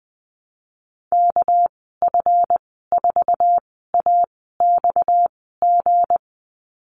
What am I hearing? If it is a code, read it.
Morse code, heard as a keyed tone, about 19 wpm, spaced as for 14 wpm, 700 Hz. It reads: KF4AXG
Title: 231 Lesson 37g - Letter x - Call Signs 20wpm